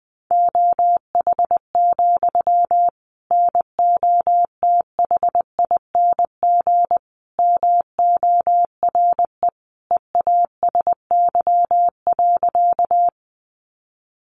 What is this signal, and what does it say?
Morse code, heard as a keyed tone, about 20 wpm, 700 Hz. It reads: OH, NOTHING MORE EASY.
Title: Sentences from Top 700 Words 20wpm